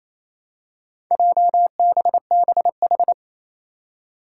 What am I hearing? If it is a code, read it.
Morse code, heard as a keyed tone, about 27 wpm, 700 Hz. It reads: JBBH
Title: Four Challenging Letter-Number Combinations 28wpm